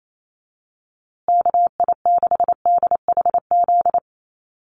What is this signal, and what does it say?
Morse code, heard as a keyed tone, about 28 wpm, 700 Hz. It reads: KI6DHZ
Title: USA Call Signs Repeated 2x 28wpm